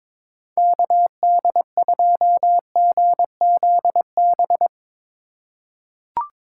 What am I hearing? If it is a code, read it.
Morse code, heard as a keyed tone, about 22 wpm, 700 Hz. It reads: KD2GZB
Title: USA Call Signs - 6 Characters - 4x Character Spacing - 22wpm